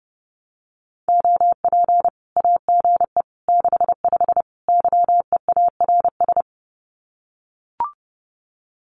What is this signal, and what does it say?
Morse code, heard as a keyed tone, about 30 wpm, 700 Hz. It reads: OP AGE 65 YEARS
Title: Common QSO Phrases 30wpm